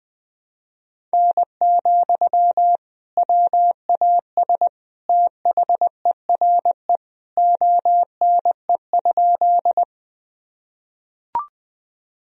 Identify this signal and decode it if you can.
Morse code, heard as a keyed tone, about 20 wpm, 700 Hz. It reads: N, WAS THERE ONE?
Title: Sentences from Top 100 Words and Abbreviations 20wpm